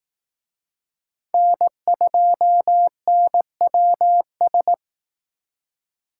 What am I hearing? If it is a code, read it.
Morse code, heard as a keyed tone, about 18 wpm, 700 Hz. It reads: N2NWS
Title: Speed Racing - USA 1x3 Call Signs 15wpm